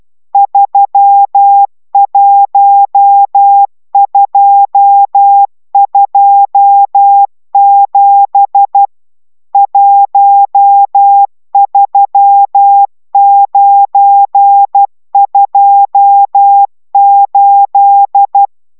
Morse code, heard as a keyed tone, about 12 wpm, 800 Hz. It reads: 31227 13928